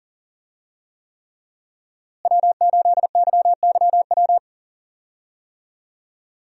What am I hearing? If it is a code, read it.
Morse code, heard as a keyed tone, about 40 wpm, 700 Hz. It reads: W8YYW
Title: USA Call Signs - 5 Characters - Repeated 3x - 40wpm